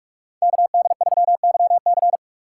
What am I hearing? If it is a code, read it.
Morse code, heard as a keyed tone, about 45 wpm, 700 Hz. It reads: KD3YC